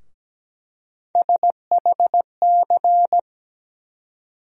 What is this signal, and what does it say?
Morse code, heard as a keyed tone, about 17 wpm, 700 Hz. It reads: SHC
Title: USA Call Sign Postfixes - 3 Letters 17wpm